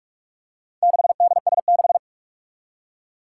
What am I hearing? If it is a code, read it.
Morse code, heard as a keyed tone, about 44 wpm, 700 Hz. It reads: 6DS6
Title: Four Challenging Letter-Number Combinations 45wpm